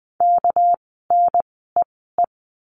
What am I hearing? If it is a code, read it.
Morse code, heard as a keyed tone, about 20 wpm, spaced as for 15 wpm, 700 Hz. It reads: KNEE
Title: 199 Lesson 32f - Letter k - Words 20wpm